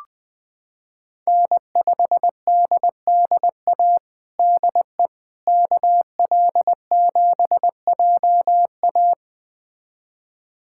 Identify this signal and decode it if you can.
Morse code, heard as a keyed tone, about 20 wpm, 700 Hz. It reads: N5DDA DE KL7JA
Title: USA Callsign DE Callsign - 5 Characters - 20wpm